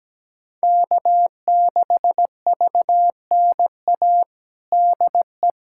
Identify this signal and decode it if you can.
Morse code, heard as a keyed tone, about 17 wpm, 700 Hz. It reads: K6VNA DE
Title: USA Callsign DE Callsign - 5 Characters - 17wpm